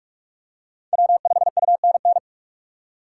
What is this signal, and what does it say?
Morse code, heard as a keyed tone, about 46 wpm, 700 Hz. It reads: W5UNN
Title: USA Call Signs - 5 Characters - 4x Character Spacing - 45wpm